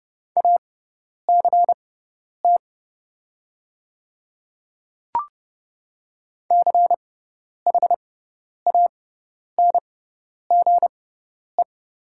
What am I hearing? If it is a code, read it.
Morse code, heard as a keyed tone, about 30 wpm, spaced as for 10 wpm, 700 Hz. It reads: ACT CHANGE
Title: 6x Character Spacing - Top 200 Words - 30wpm